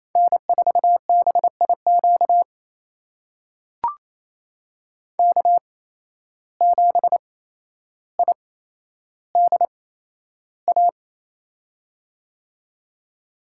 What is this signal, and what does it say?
Morse code, heard as a keyed tone, about 28 wpm, spaced as for 12 wpm, 700 Hz. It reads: N4BIQ K7IDA
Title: USA Call Signs - 5 Characters - 8x Character Spacing - 28wpm